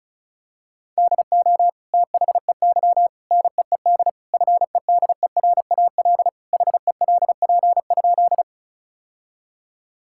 Morse code, heard as a keyed tone, about 36 wpm, 700 Hz. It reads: DO THEY NEED FEDERAL HELP?
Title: Sentences from Top 1000 Words - Review Part 5 - 35wpm